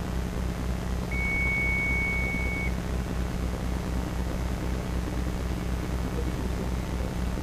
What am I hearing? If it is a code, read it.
Dial-up modem handshake.